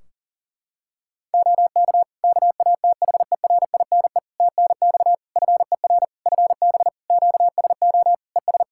Morse code, heard as a keyed tone, about 40 wpm, 700 Hz. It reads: OK KATHERINE TNX FER FB QSO ES
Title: K1USN SST QSO Practice 40wpm